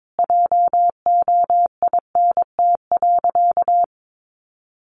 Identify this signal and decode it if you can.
Morse code, heard as a keyed tone, about 22 wpm, 700 Hz. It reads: JOINT.
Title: Sentences from Top 1300 Words 22wpm